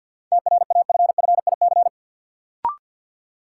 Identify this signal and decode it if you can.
Morse code, heard as a keyed tone, about 53 wpm, 700 Hz. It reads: TRAFFIC
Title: Sentences from Top 1300 Words - New Word Drill - 50wpm